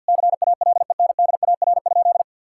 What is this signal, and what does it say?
Morse code, heard as a keyed tone, about 50 wpm, 700 Hz. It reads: CALENDAR?
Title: Sentences from Top 1600 Words 50wpm